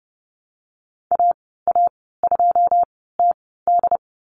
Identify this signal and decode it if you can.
Morse code, heard as a keyed tone, about 30 wpm, spaced as for 17 wpm, 700 Hz. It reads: AA2TD
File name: Lesson 15g - Number 5 - Call Signs 30wpm